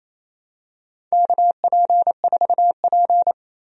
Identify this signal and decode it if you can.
Morse code, heard as a keyed tone, about 28 wpm, 700 Hz. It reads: KP4P
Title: USA Call Signs - 4 Characters - 28wpm